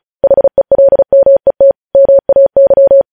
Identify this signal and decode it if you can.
Morse code, heard as a keyed tone, about 36 wpm, 550 Hz. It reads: HELMET MAY
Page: News headlines in Morse code at 35 wpm